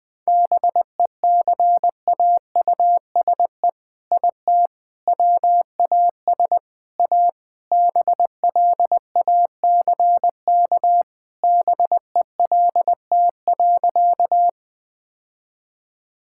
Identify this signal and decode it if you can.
Morse code, heard as a keyed tone, about 20 wpm, 700 Hz. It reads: BECAUSE IT WAS A BLACK BELT.
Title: All-Age Appropriate Jokes - 20wpm